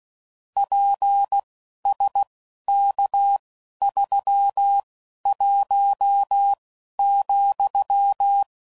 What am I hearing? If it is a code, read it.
Morse code, heard as a keyed tone, about 16 wpm, spaced as for 11 wpm, 800 Hz. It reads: PSK31,
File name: CW Code Groups @10WPM / QSOs / QSO kc7sac